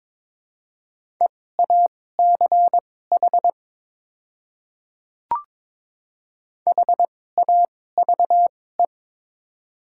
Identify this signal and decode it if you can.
Morse code, heard as a keyed tone, about 23 wpm, spaced as for 16 wpm, 690 Hz. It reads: EACH HAVE
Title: 2x Character Spacing - Top 200 Words - 22wpm